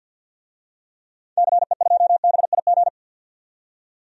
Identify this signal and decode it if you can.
Morse code, heard as a keyed tone, about 51 wpm, 700 Hz. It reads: CE2BIG
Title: DXCC-100 Call Signs - 6 Characters - 50wpm